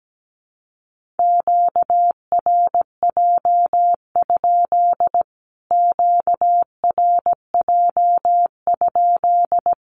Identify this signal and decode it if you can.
Morse code, heard as a keyed tone, about 17 wpm, 700 Hz. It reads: QRJ? QRJ?
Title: Q-Codes Repeated 3x 17wpm